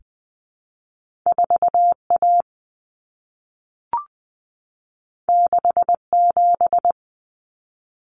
Morse code, heard as a keyed tone, about 20 wpm, 700 Hz. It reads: 4A 67
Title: Common Cut Numbers - 2-Digit 20wpm